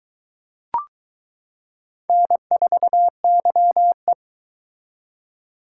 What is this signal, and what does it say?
Morse code, heard as a keyed tone, about 22 wpm, 700 Hz. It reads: N4YE